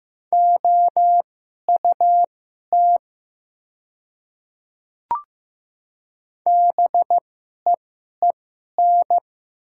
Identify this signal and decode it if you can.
Morse code, heard as a keyed tone, about 15 wpm, spaced as for 11 wpm, 700 Hz. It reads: OUT BEEN
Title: 2x Character Spacing - Top 200 Words - 15wpm